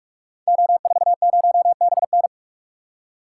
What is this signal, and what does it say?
Morse code, heard as a keyed tone, about 44 wpm, 700 Hz. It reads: O40BN
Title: Code Groups - 5 Letter:Numbers - 45wpm